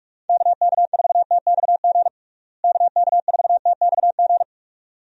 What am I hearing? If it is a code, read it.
Morse code, heard as a keyed tone, about 46 wpm, 700 Hz. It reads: KK4TXG KK4TXG
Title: USA Call Signs - 6 Characters - Repeated 2x - 45wpm